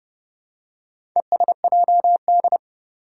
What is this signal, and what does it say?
Morse code, heard as a keyed tone, about 30 wpm, 700 Hz. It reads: ESJD
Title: Four Challenging Letter-Number Combinations 30wpm